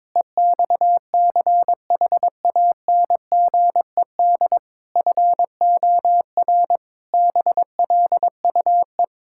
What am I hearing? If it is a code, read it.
Morse code, heard as a keyed tone, about 22 wpm, 700 Hz. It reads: EXCHANGED FOR BLUE
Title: Pangram Challenge v2 - 22wpm